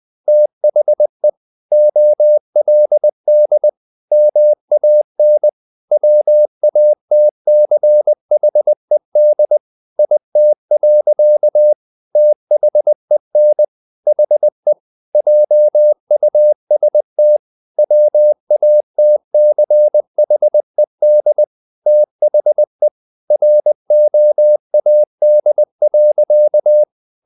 Morse code, heard as a keyed tone, about 20 wpm, 600 Hz. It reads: THE OLD MAN WATCHED IT. THEN HE JUST WATCHED THE ROAD.
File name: the old man and the sea part 2 20wpm